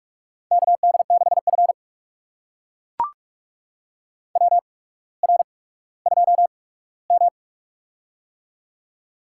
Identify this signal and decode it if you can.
Morse code, heard as a keyed tone, about 44 wpm, spaced as for 19 wpm, 700 Hz. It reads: KD6F WR2M
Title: USA Call Signs - 4 Characters - 8x Character Spacing - 45wpm